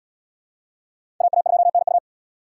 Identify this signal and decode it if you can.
Morse code, heard as a keyed tone, about 77 wpm, 690 Hz. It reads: AA2NU